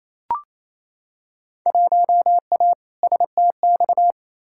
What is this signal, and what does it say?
Morse code, heard as a keyed tone, about 28 wpm, 700 Hz. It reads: 1A STX
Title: Field Day Exchanges Encore - 28wpm